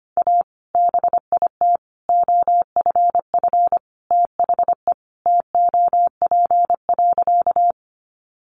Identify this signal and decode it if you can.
Morse code, heard as a keyed tone, about 25 wpm, 700 Hz. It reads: A BIT OFF THE TOP.
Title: Sentences from Top 900 Words 25wpm